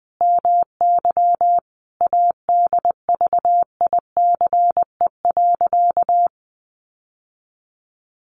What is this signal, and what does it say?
Morse code, heard as a keyed tone, about 20 wpm, 700 Hz. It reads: MY ADVICE.
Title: Sentences from Top 1100 Words 20wpm